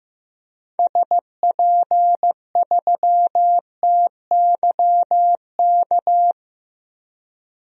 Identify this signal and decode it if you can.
Morse code, heard as a keyed tone, about 15 wpm, 700 Hz. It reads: SP3TYK
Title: DXCC-100 Call Signs - 6 Characters - 15wpm